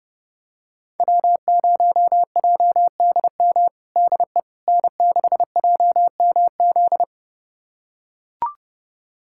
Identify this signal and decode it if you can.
Morse code, heard as a keyed tone, about 31 wpm, 700 Hz. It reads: W0JDM DE N6JMZ
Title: USA Callsign DE Callsign 30wpm